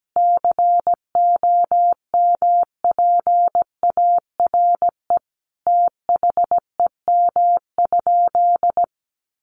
Morse code, heard as a keyed tone, about 17 wpm, 700 Hz. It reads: COMPARE THEM?
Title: Sentences from Top 1300 Words 17wpm